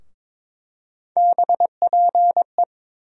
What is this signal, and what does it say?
Morse code, heard as a keyed tone, about 22 wpm, 700 Hz. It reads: BPE
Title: USA Call Sign Postfixes - 3 Letters 22wpm